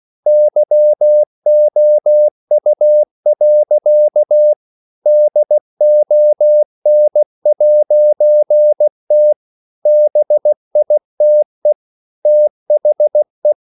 Morse code, heard as a keyed tone, about 16 wpm, 600 Hz. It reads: YOU. DON'T BITE THE
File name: CW / Practice / EnglishSayings16